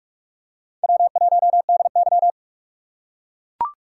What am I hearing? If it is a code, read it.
Morse code, heard as a keyed tone, about 44 wpm, 700 Hz. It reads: W1DY